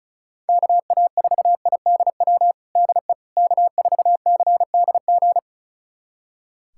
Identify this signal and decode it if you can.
Morse code, heard as a keyed tone, about 34 wpm, 700 Hz. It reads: KA4IDW DE K4CDG
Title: USA Callsign DE Callsign 35wpm